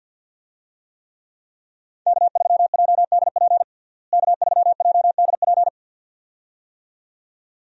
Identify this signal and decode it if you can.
Morse code, heard as a keyed tone, about 52 wpm, 700 Hz. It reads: K3JDP K3JDP
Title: USA Call Signs - 5 Characters - Repeated 2x - 50wpm